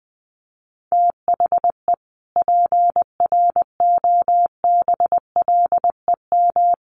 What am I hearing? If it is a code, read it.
Morse code, heard as a keyed tone, about 19 wpm, 700 Hz. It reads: THE PROBLEM
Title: Sets of 5 Words Encore2 20wpm